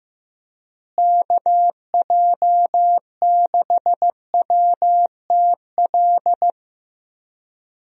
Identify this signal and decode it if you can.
Morse code, heard as a keyed tone, about 15 wpm, 700 Hz. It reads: KJ6WTL
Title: USA Call Signs - 6 Characters - Repeated 3x - 15wpm